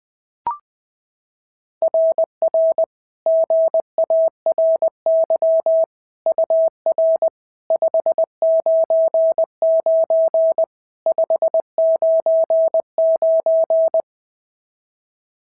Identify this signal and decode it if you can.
Morse code, heard as a keyed tone, about 20 wpm, 650 Hz. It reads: RR GARY UR 599 599
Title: SOTA QSO Practice 20wpm